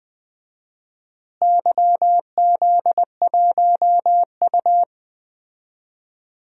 Morse code, heard as a keyed tone, about 20 wpm, 700 Hz. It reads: YZ1U
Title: DXCC-100 Call Signs - 4 Characters - Repeated 3x - 20wpm